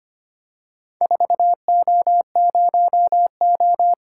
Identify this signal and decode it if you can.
Morse code, heard as a keyed tone, about 25 wpm, 700 Hz. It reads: 4O0O
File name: practice / 25wpm / Four Letter-Number Combinations - 25wpm